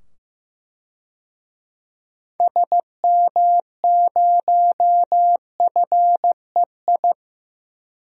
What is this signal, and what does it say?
Morse code, heard as a keyed tone, about 15 wpm, 700 Hz. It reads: SM0FEI
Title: DXCC-100 Call Signs - 6 Characters - 15wpm